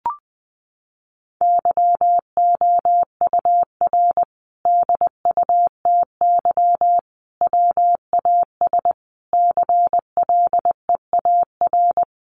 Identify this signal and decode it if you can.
Morse code, heard as a keyed tone, about 20 wpm, 700 Hz. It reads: YOUR DUTY WAS CLEAR
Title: Sentences from Top 1500 Words 20wpm